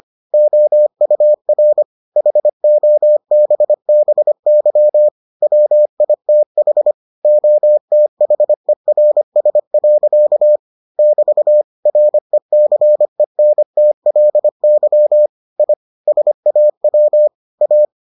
Morse code, heard as a keyed tone, about 25 wpm, 600 Hz. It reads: OUR HOBBY WITH OTHERS. <BT> RECENTLY I SAW A